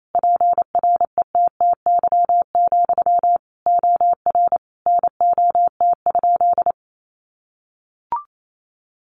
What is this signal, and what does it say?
Morse code, heard as a keyed tone, about 28 wpm, 700 Hz. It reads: PRETTY, OR NOT?
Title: Sentences from Top 1000 Words 28wpm